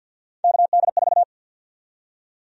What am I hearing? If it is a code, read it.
Morse code, heard as a keyed tone, about 48 wpm, 700 Hz. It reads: KD4
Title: USA Call Sign Prefixes - 50wpm